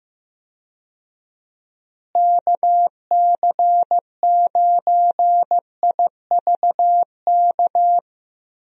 Morse code, heard as a keyed tone, about 15 wpm, 700 Hz. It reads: KC9IVK
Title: USA Call Signs - 6 Characters - Repeated 3x - 15wpm